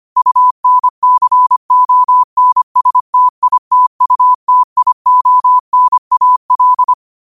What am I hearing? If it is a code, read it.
Morse code, heard as a keyed tone, about 25 wpm, 1000 Hz. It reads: UNCONSTITUTIONAL